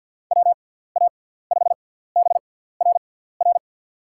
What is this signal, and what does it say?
Morse code, heard as a keyed tone, about 48 wpm, spaced as for 17 wpm, 690 Hz. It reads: WA5BRR
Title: USA Call Signs - 6x Character Spacing - 50wpm